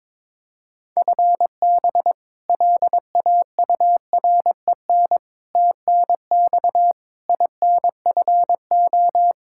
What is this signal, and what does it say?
Morse code, heard as a keyed tone, about 21 wpm, 700 Hz. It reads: FB LAUREN TNX INFO
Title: K1USN SST QSO Practice 22wpm